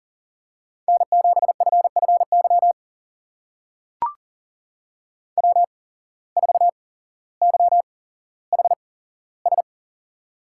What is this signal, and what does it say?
Morse code, heard as a keyed tone, about 39 wpm, spaced as for 17 wpm, 700 Hz. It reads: N7FFY W4YHS
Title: USA Call Signs - 5 Characters - 8x Character Spacing - 40wpm